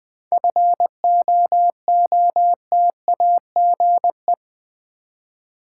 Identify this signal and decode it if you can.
Morse code, heard as a keyed tone, about 20 wpm, 690 Hz. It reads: FOOTAGE
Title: 7 Letter Words - First 3 Letters Given - 20wpm